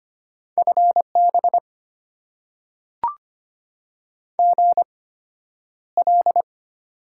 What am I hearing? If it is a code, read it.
Morse code, heard as a keyed tone, about 25 wpm, spaced as for 11 wpm, 690 Hz. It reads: FB GL
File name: Top 90 QSO Elements - 8x Character Spacing - 25wpm